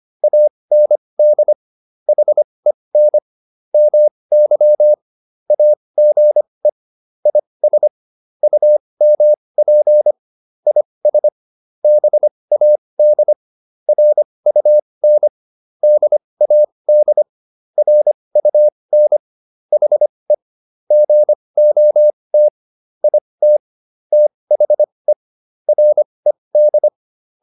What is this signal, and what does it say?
Morse code, heard as a keyed tone, about 25 wpm, spaced as for 20 wpm, 600 Hz. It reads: AND HEN MY AGE IS UMP IS BAD RUN DAD RUN HE GOT IT THE RED